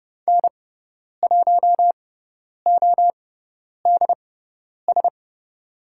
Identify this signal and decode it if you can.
Morse code, heard as a keyed tone, about 30 wpm, spaced as for 10 wpm, 700 Hz. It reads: N1ODS
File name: Lesson 17g - Letter u - Call Signs 30wpm